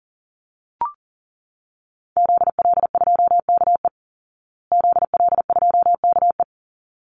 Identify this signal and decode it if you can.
Morse code, heard as a keyed tone, about 40 wpm, 700 Hz. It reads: ZL2KE ZL2KE
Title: Call Signs Repeated 3x - 40wpm